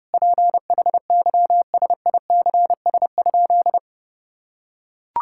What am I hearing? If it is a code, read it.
Morse code, heard as a keyed tone, about 28 wpm, 700 Hz. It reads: PHYSICS?
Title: Sentences from Top 1000 Words - Review Part 5 - 30wpm